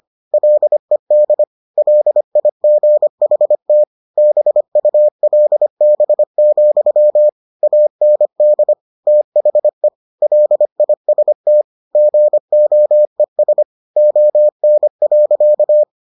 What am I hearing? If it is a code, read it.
Morse code, heard as a keyed tone, about 25 wpm, 600 Hz. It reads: LED LIGHT BULB, AND THE LIST GOES ON.